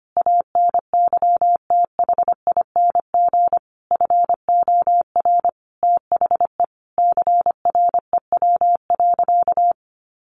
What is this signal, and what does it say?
Morse code, heard as a keyed tone, about 25 wpm, 700 Hz. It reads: ANYTHING FOR THE CREW.
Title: Sentences from Top 1500 Words 25wpm